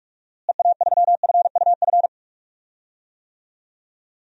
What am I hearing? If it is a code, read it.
Morse code, heard as a keyed tone, about 41 wpm, 700 Hz. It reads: EA3FUF
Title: DXCC-100 Call Signs - 6 Characters - Repeated 3x - 45wpm